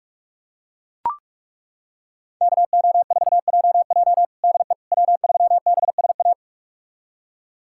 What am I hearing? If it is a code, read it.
Morse code, heard as a keyed tone, about 41 wpm, 700 Hz. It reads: KO4JJ DE W3BSA
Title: USA Callsign DE Callsign - 5 Characters - 45wpm